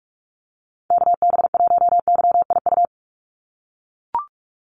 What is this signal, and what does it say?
Morse code, heard as a keyed tone, about 44 wpm, 700 Hz. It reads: KB1YIU
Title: USA Call Signs 45wpm